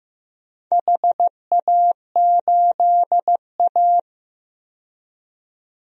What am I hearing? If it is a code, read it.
Morse code, heard as a keyed tone, about 15 wpm, 700 Hz. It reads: HA8A
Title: DXCC-100 Call Signs - 4 Characters - 15wpm